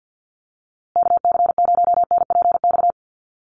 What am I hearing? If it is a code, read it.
Morse code, heard as a keyed tone, about 48 wpm, 700 Hz. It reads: KC0NPX
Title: USA Call Signs Repeated 2x 50wpm